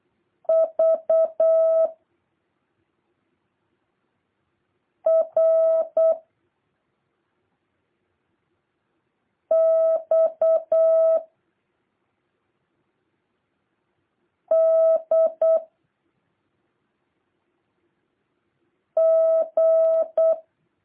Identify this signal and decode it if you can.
Morse code, heard as a keyed tone, about 8 wpm, 650 Hz. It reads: V R X D G
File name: jh8cbh / 2023cw